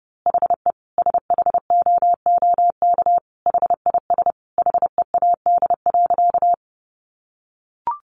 Morse code, heard as a keyed tone, about 30 wpm, 700 Hz. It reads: HE SHOOK HIS HEAD.
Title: A Sound Of Thunder 30wpm